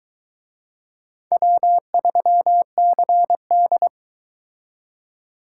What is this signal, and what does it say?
Morse code, heard as a keyed tone, about 23 wpm, 700 Hz. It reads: W3CD